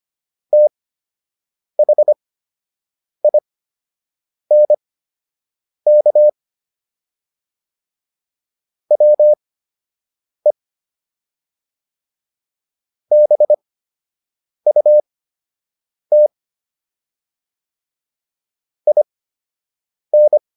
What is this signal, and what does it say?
Morse code, heard as a keyed tone, about 25 wpm, spaced as for 7 wpm, 600 Hz. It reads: THINK WE BUT IN